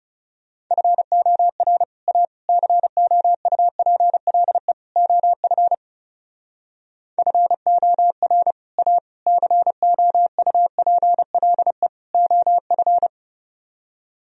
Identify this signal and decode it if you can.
Morse code, heard as a keyed tone, about 40 wpm, 700 Hz. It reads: FOR A COUPLE OF FOR A COUPLE OF